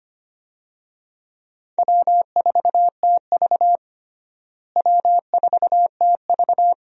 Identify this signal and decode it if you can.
Morse code, heard as a keyed tone, about 25 wpm, 700 Hz. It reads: W4TV W4TV
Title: USA Call Signs - 4 Characters - Repeated 2x - 25wpm